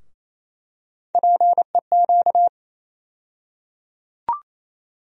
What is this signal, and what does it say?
Morse code, heard as a keyed tone, about 28 wpm, 700 Hz. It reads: PEQ
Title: USA Call Sign Postfixes - 3 Letters 28wpm